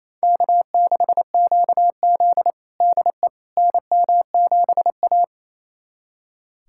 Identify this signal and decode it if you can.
Morse code, heard as a keyed tone, about 28 wpm, 700 Hz. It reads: K6QZ DE NM7A
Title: USA Callsign DE Callsign - 4 Characters - 28wpm